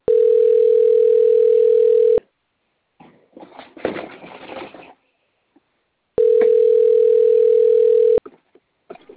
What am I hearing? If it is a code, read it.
Ringback tone (North America), 440 + 480 Hz.
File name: out- - - - - .8071